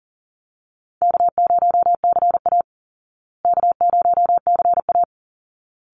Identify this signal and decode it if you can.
Morse code, heard as a keyed tone, about 40 wpm, 700 Hz. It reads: K0CA K0CA
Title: Easy Call Signs Repeated 2x - 40wpm